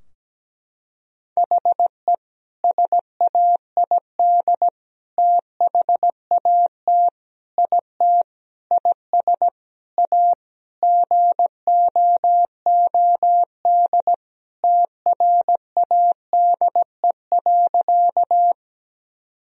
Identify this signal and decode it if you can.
Morse code, heard as a keyed tone, about 17 wpm, 700 Hz. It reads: HE SAID THAT IT IS A GOOD TRADE.
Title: Sentences from Top 1000 Words - Review Part 4 - 17wpm